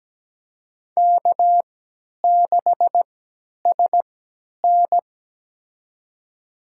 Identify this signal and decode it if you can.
Morse code, heard as a keyed tone, about 17 wpm, spaced as for 10 wpm, 700 Hz. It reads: K6SN
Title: USA Call Signs - 4 Characters - 3x Character Spacing - 17wpm